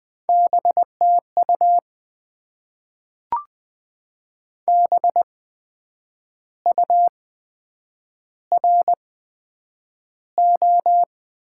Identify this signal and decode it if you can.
Morse code, heard as a keyed tone, about 20 wpm, spaced as for 7 wpm, 700 Hz. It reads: B TU BURO
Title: Top 90 QSO Elements - 8x Character Spacing - 20wpm